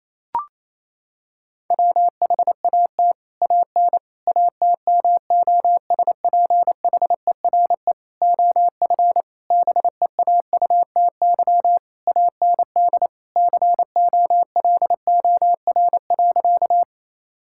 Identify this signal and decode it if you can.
Morse code, heard as a keyed tone, about 28 wpm, 700 Hz. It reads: WHAT AN ATMOSPHERE OF BEAUTY AND COLOR.
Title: Sentences from Top 1600 Words 28wpm